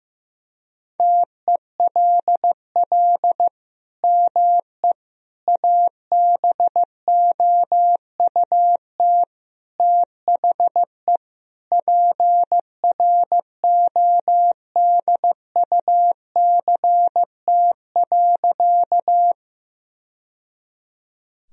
Morse code, heard as a keyed tone, about 15 wpm, 700 Hz. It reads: TELL ME ABOUT THE PRODUCT.
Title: Sentences from Top 1000 Words - Review Part 2 - 15wpm